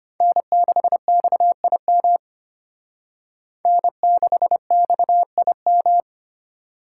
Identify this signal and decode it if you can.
Morse code, heard as a keyed tone, about 32 wpm, 700 Hz. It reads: N6XIM N6XIM